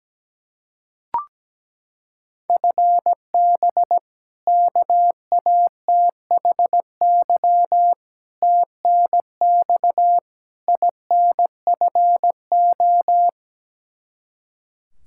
Morse code, heard as a keyed tone, about 17 wpm, 700 Hz. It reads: FB KATHY TNX INFO
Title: Practice QSO - Exchange 3 - More Info - 17wpm